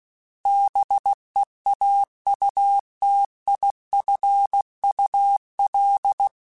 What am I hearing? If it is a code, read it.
Morse code, heard as a keyed tone, about 16 wpm, 800 Hz. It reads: BEAUTIFUL
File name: Misc / Misc-mp3s / BIG @15